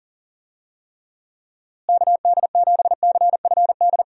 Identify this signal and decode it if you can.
Morse code, heard as a keyed tone, about 40 wpm, 700 Hz. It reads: KD7CFD
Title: USA Call Signs - 6 Characters - Repeated 3x - 40wpm